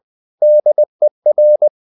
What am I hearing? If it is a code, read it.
Morse code, heard as a keyed tone, about 20 wpm, 600 Hz. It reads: DER